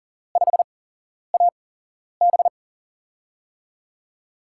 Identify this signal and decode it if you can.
Morse code, heard as a keyed tone, about 40 wpm, spaced as for 11 wpm, 700 Hz. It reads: 5AB
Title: USA Call Signs - 8x Character Spacing - 40wpm